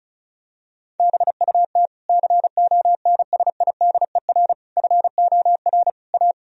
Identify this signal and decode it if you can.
Morse code, heard as a keyed tone, about 36 wpm, 700 Hz. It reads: BUT CONSIDER FOR A
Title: Sentences from Top 1000 Words - Review Part 4 - 35wpm